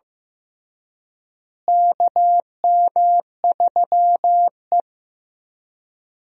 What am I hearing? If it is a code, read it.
Morse code, heard as a keyed tone, about 15 wpm, 700 Hz. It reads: KM3E
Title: USA Call Signs - 4 Characters - Repeated 3x - 15wpm